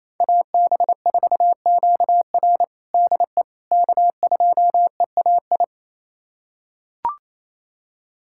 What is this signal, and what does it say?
Morse code, heard as a keyed tone, about 28 wpm, 700 Hz. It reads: AB4QR DE K2EAI
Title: USA Callsign DE Callsign - 5 Characters - 28wpm